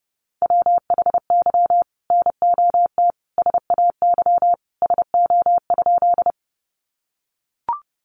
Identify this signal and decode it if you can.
Morse code, heard as a keyed tone, about 30 wpm, 700 Hz. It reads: WHY NOT SAY SO?
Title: Sentences from Top 600 Words 30wpm